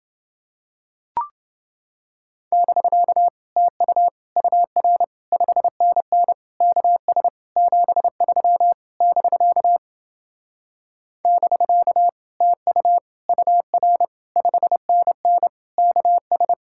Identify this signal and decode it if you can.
Morse code, heard as a keyed tone, about 29 wpm, 700 Hz. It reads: <BK> TU UR 5NN KS 73 <BK> <BK> TU UR 5NN KS